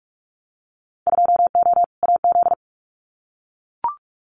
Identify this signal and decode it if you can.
Morse code, heard as a keyed tone, about 44 wpm, 700 Hz. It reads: 2O AZ
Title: Winter Field Day Exchanges 45wpm